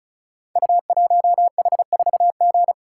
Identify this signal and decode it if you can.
Morse code, heard as a keyed tone, about 36 wpm, 690 Hz. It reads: U1H4G
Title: Code Groups - 5 Letter:Numbers - 35wpm